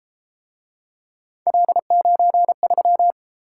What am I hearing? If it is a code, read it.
Morse code, heard as a keyed tone, about 31 wpm, 700 Hz. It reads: L93